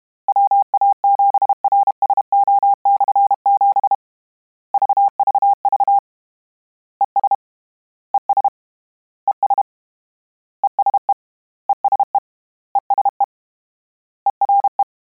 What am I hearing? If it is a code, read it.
Morse code, heard as a keyed tone, about 32 wpm, 800 Hz. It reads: WA7RSO/7 VVV ES ES ES ESE ESE ESE ERE
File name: Combinations / Combinations-mp3s / Combinations@30